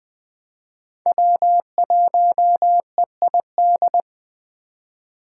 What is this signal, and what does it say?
Morse code, heard as a keyed tone, about 20 wpm, 700 Hz. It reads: W1EID
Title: USA Call Signs - 5 Characters - 2x Character Spacing - 20wpm